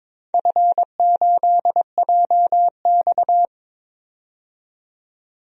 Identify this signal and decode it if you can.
Morse code, heard as a keyed tone, about 22 wpm, 700 Hz. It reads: F8JX
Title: DXCC-100 Call Signs - 4 Characters - 22wpm